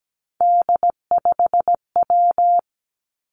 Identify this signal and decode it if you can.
Morse code, heard as a keyed tone, about 17 wpm, 700 Hz. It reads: D5W
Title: Three Challenging Letter-Number Combinations 17wpm